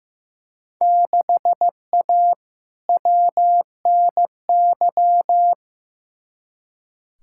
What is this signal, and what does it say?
Morse code, heard as a keyed tone, about 15 wpm, 700 Hz. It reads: 6A WNY
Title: Field Day Exchanges Encore - 15wpm